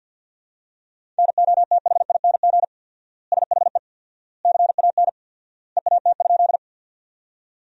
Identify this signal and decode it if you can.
Morse code, heard as a keyed tone, about 50 wpm, 700 Hz. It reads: NOTHING SHE CAN EAT?
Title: 4x Word-Spacing - Sentences from Top 500 Words - 50wpm